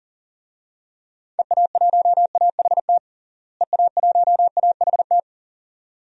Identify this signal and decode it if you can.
Morse code, heard as a keyed tone, about 40 wpm, 700 Hz. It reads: EA1AHT EA1AHT
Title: DXCC-100 Call Signs - 6 Characters - Repeated 2x - 40wpm